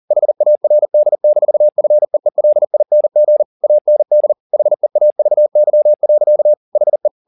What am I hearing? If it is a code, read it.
Morse code, heard as a keyed tone, about 40 wpm, 600 Hz. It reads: HARD-FEELING AND HEAVY. HE
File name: the old man and the sea 2 40wpm